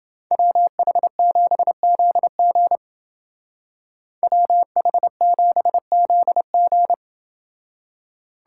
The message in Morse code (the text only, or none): WH7ZG WH7ZG